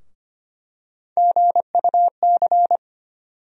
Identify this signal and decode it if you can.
Morse code, heard as a keyed tone, about 25 wpm, 700 Hz. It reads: GUC
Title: USA Call Sign Postfixes - 3 Letters 25wpm